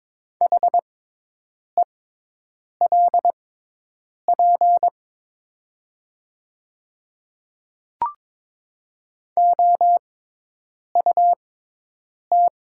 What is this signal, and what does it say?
Morse code, heard as a keyed tone, about 22 wpm, spaced as for 8 wpm, 700 Hz. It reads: HELP OUT
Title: 6x Character Spacing - Top 200 Words - 22wpm